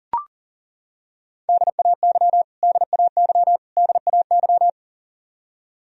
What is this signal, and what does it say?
Morse code, heard as a keyed tone, about 40 wpm, 700 Hz. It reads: DAY DAY DAY
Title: Top 100 Words Repeated 40wpm